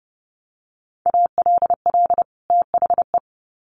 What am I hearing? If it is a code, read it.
Morse code, heard as a keyed tone, about 30 wpm, 700 Hz. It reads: ALL THE
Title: Sets of 2 Words 30wpm